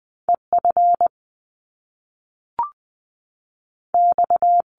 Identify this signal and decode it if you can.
Morse code, heard as a keyed tone, about 20 wpm, 700 Hz. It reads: E F X
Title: Top Two Letter Combinations in Words 20wpm